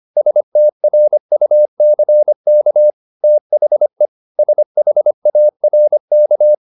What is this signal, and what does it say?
Morse code, heard as a keyed tone, about 25 wpm, 600 Hz. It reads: STRUCK THE SHARK